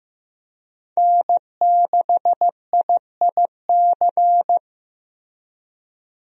Morse code, heard as a keyed tone, about 15 wpm, 700 Hz. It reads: N6IIC
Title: USA Call Signs - 5 Characters - Repeated 3x - 15wpm